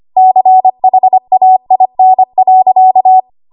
Morse code, heard as a keyed tone, about 25 wpm, 750 Hz. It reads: CHAIN.
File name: Archive / 25 WPM / 25WPM